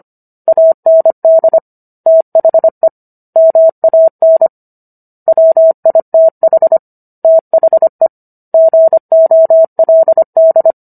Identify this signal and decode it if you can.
Morse code, heard as a keyed tone, about 25 wpm, 650 Hz. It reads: AND THE MAN WITH THE GOLD